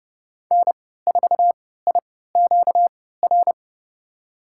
Morse code, heard as a keyed tone, about 32 wpm, spaced as for 17 wpm, 700 Hz. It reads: N4IQR
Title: USA Call Signs - 5 Characters - 3x Character Spacing - 30wpm